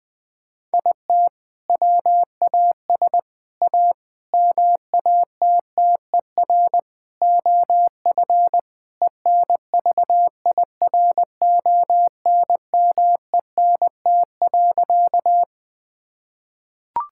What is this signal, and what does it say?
Morse code, heard as a keyed tone, about 20 wpm, 700 Hz. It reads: IT WAS A MATTER OF ENVIRONMENT.
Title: Sentences from Top 1000 Words - Review Part 5 - 20wpm